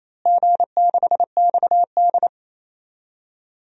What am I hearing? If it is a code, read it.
Morse code, heard as a keyed tone, about 27 wpm, 700 Hz. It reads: G6XD
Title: DXCC-100 Call Signs - 4 Characters - Repeated 3x - 28wpm